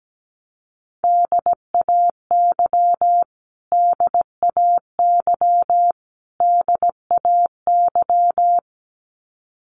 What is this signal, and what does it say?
Morse code, heard as a keyed tone, about 17 wpm, 700 Hz. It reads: DAY DAY DAY
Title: Top 300 Words Repeated 17wpm